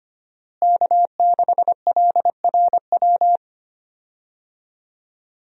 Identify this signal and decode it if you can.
Morse code, heard as a keyed tone, about 25 wpm, 700 Hz. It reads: K6LRW
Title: USA Call Signs - 5 Characters - Repeated 2x - 25wpm